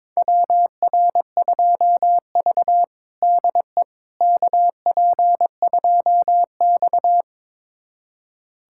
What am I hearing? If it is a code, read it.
Morse code, heard as a keyed tone, about 21 wpm, 700 Hz. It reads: WR2V DE KP2X
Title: USA Callsign DE Callsign - 4 Characters - 22wpm